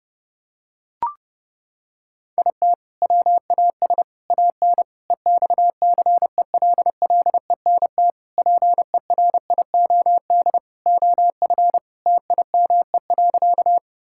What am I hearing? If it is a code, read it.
Morse code, heard as a keyed tone, about 32 wpm, 700 Hz. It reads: IT WAS AN EXCELLENT PERIOD OF TIME.
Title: Sentences from Top 1000 Words - Review Part 5 - 30wpm